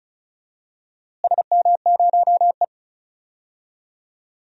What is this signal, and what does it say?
Morse code, heard as a keyed tone, about 36 wpm, 700 Hz. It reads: SM0E
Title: DXCC-100 Call Signs - 4 Characters - Repeated 3x - 35wpm